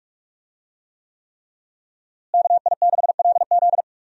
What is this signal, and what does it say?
Morse code, heard as a keyed tone, about 41 wpm, 700 Hz. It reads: KI6LZ
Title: USA Call Signs - 5 Characters - Repeated 3x - 45wpm